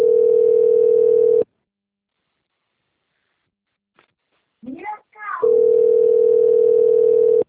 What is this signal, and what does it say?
Ringback tone (North America), 440 + 480 Hz.
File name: rg-1600-unknown- - - .38072